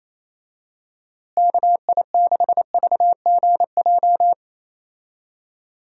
Morse code, heard as a keyed tone, about 28 wpm, 700 Hz. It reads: KI6VGJ
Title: USA Call Signs - 6 Characters - Repeated 2x - 28wpm